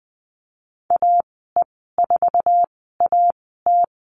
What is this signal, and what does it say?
Morse code, heard as a keyed tone, about 20 wpm, spaced as for 15 wpm, 700 Hz. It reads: AE4AT
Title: 048 Lesson 9g - Number 4 - Call Signs 20wpm